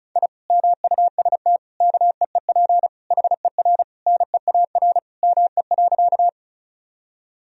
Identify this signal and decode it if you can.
Morse code, heard as a keyed tone, about 36 wpm, 700 Hz. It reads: I MUST KEEP HER NEAR ME.
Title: Sentences from Top 1000 Words - Review Part 1 - 35wpm